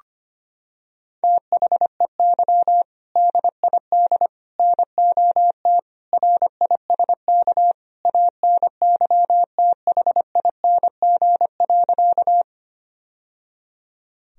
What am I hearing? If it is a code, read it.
Morse code, heard as a keyed tone, about 25 wpm, 700 Hz. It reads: THEY DID NOT RISK ANYTHING.
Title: Sentences from Top 1000 Words - Review Part 4 - 25wpm